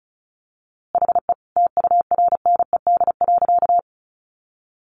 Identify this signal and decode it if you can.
Morse code, heard as a keyed tone, about 34 wpm, 700 Hz. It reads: HE TURNED.
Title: A Sound Of Thunder 35wpm